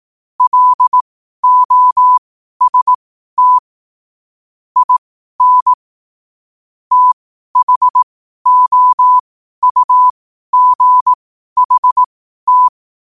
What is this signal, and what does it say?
Morse code, heard as a keyed tone, about 18 wpm, spaced as for 12 wpm, 1000 Hz. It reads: LOST IN THOUGHT
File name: qotdfiles / 12wpm